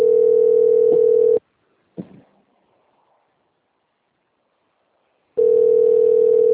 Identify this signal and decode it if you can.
Ringback tone (North America), 440 + 480 Hz.